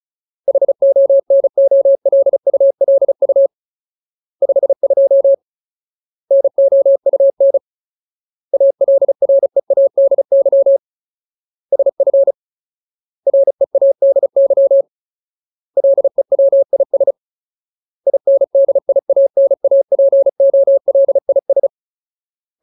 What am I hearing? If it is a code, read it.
Morse code, heard as a keyed tone, about 34 wpm, 550 Hz. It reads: HONOLULU 52 NOUN ALREADY SF READY LEWIS INDIANAPOLIS
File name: Ultimate Word List / 35WPM Ultimate